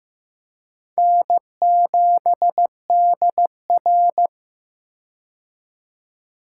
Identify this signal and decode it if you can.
Morse code, heard as a keyed tone, about 16 wpm, 700 Hz. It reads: N7DR
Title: K1USN SST QSO Practice - 15wpm